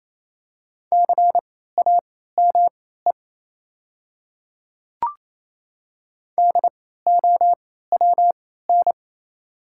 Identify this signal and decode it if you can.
Morse code, heard as a keyed tone, about 28 wpm, spaced as for 16 wpm, 700 Hz. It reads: CAME DOWN
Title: 3x Character Spacing - Top 200 Words - 28wpm